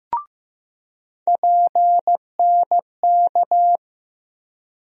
Morse code, heard as a keyed tone, about 16 wpm, 700 Hz. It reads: PNK
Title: USA Call Sign Postfixes - 3 Letters 15wpm